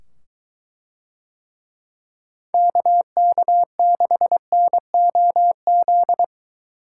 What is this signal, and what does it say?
Morse code, heard as a keyed tone, about 21 wpm, 700 Hz. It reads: KK6NOZ